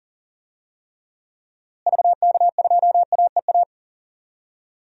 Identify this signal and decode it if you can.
Morse code, heard as a keyed tone, about 40 wpm, 700 Hz. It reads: VK2AEA
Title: DXCC-100 Call Signs - 6 Characters - Repeated 2x - 40wpm